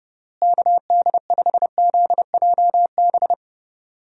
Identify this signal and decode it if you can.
Morse code, heard as a keyed tone, about 32 wpm, 700 Hz. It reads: KD5ZJB
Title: USA Call Signs - 6 Characters - Repeated 3x - 30wpm